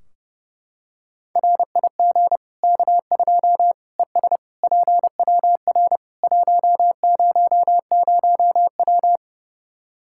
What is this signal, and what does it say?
Morse code, heard as a keyed tone, about 32 wpm, 700 Hz. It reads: RIG K2 ES PWR 100W
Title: Practice QSO - Exchange 2 - Info - 30wpm